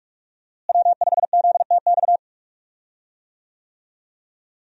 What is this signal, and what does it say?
Morse code, heard as a keyed tone, about 46 wpm, 700 Hz. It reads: W5ZTX
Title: USA Call Signs - 5 Characters - Repeated 2x - 45wpm